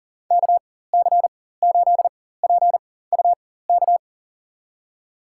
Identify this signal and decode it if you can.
Morse code, heard as a keyed tone, about 40 wpm, spaced as for 19 wpm, 700 Hz. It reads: KC8PUK
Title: USA Call Signs - 6 Characters - 4x Character Spacing - 40wpm